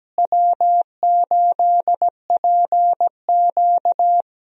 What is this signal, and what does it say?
Morse code, heard as a keyed tone, about 17 wpm, 700 Hz. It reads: W8PQ
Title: USA Call Signs - 4 Characters - Repeated 2x - 17wpm